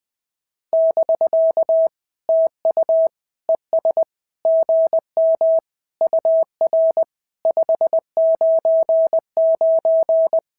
Morse code, heard as a keyed tone, about 20 wpm, 650 Hz. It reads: <BK> TU ES GM UR 599
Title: SOTA QSO Practice 20wpm